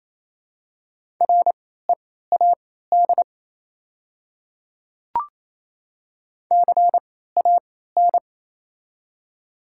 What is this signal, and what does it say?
Morse code, heard as a keyed tone, about 28 wpm, spaced as for 16 wpm, 700 Hz. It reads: READ CAN
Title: 3x Character Spacing - Top 200 Words - 28wpm